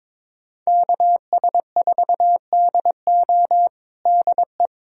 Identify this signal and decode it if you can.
Morse code, heard as a keyed tone, about 22 wpm, 700 Hz. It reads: KS4DO DE
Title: USA Callsign DE Callsign 22wpm